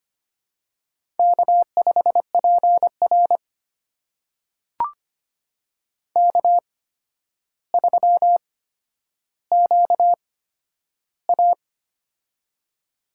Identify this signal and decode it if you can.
Morse code, heard as a keyed tone, about 25 wpm, spaced as for 11 wpm, 700 Hz. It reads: K5PR K3QA
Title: USA Call Signs - 4 Characters - 8x Character Spacing - 25wpm